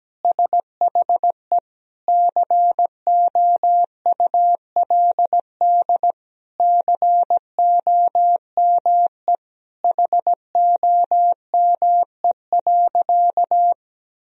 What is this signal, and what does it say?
Morse code, heard as a keyed tone, about 17 wpm, 700 Hz. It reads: SHE COULD COME HOME.
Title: Sentences from Top 1000 Words - Review Part 3 - 17wpm